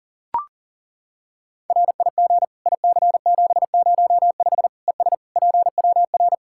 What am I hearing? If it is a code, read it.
Morse code, heard as a keyed tone, about 40 wpm, 700 Hz. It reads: RIG IC705 ES PWR
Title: K1USN SST QSO Practice 40wpm